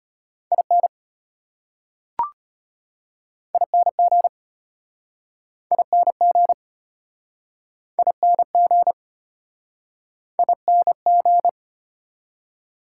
Morse code, heard as a keyed tone, about 34 wpm, 700 Hz. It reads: IN ING ING ING ING